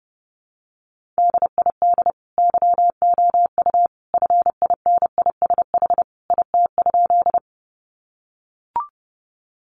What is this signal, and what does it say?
Morse code, heard as a keyed tone, about 28 wpm, 700 Hz. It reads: DID YOU FINISH IT?
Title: Sentences from Top 1200 Words 30wpm